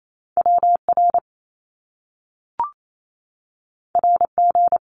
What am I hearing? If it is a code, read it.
Morse code, heard as a keyed tone, about 28 wpm, 700 Hz. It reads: WR RG
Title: Top Two Letter Combinations in Words 28wpm